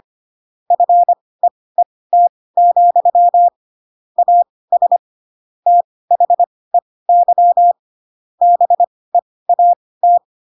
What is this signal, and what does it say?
Morse code, heard as a keyed tone, about 25 wpm, spaced as for 18 wpm, 700 Hz. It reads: FEET, AS THEY BEAT
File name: Pinocchio ch3 12 25 ik1hge